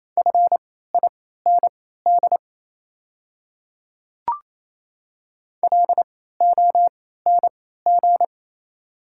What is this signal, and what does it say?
Morse code, heard as a keyed tone, about 28 wpm, spaced as for 16 wpm, 690 Hz. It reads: FIND LONG
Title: 3x Character Spacing - Top 100 Words - 28wpm